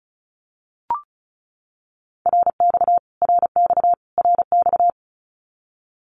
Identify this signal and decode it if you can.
Morse code, heard as a keyed tone, about 34 wpm, 700 Hz. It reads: RX RX RX
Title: Abbreviations Repeated 3x - 35wpm